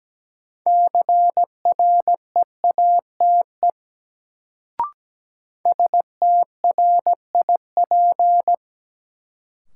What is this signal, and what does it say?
Morse code, heard as a keyed tone, about 17 wpm, 700 Hz. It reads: CREATE STRIP
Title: Sentences from Top 1600 Words - New Word Drill Alt - 17wpm